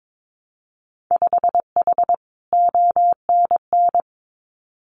Morse code, heard as a keyed tone, about 22 wpm, 700 Hz. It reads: 5H ONN
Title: Winter Field Day Exchanges Encore 22wpm